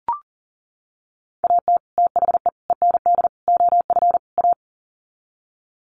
Morse code, heard as a keyed tone, about 40 wpm, 700 Hz. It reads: AT THE END OF A
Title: Sets of 5 Words 40wpm